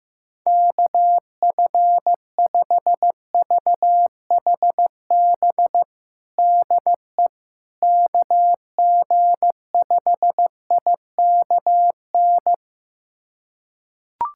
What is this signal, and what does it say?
Morse code, heard as a keyed tone, about 16 wpm, 700 Hz. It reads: KF5VHB DE KG5IKN
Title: USA Callsign DE Callsign - 6 Characters - 15wpm